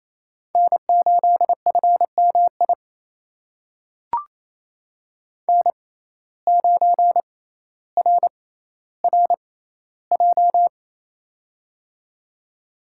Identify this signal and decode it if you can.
Morse code, heard as a keyed tone, about 28 wpm, spaced as for 14 wpm, 700 Hz. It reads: N8FMI N9RRJ
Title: USA Call Signs - 5 Characters - 6x Character Spacing - 28wpm